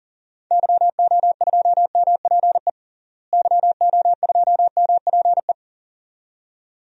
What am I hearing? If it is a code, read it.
Morse code, heard as a keyed tone, about 40 wpm, 700 Hz. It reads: YO2MPE YO2MPE
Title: DXCC-100 Call Signs - 6 Characters - Repeated 2x - 40wpm